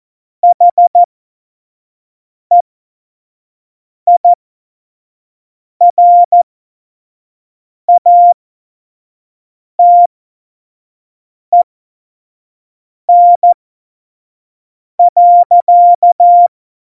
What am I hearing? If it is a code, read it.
Morse code, heard as a keyed tone, about 14 wpm, spaced as for 5 wpm, 700 Hz. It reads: HEIRATEN.
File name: Morsekurs / Witze / Witz 08. Dezember2020, 25 F70